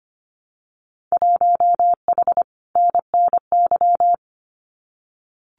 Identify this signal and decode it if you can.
Morse code, heard as a keyed tone, about 25 wpm, 700 Hz. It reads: 1H NNY
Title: Winter Field Day Exchanges 25wpm